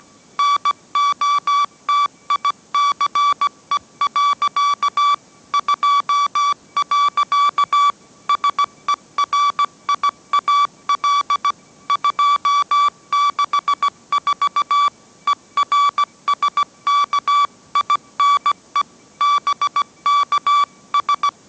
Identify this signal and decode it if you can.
Morse code, heard as a keyed tone, about 17 wpm, 1150 Hz. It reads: NOTICE. 2. SERIAL 264 ERSKINE BKS
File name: mrx / mrx080 18wpm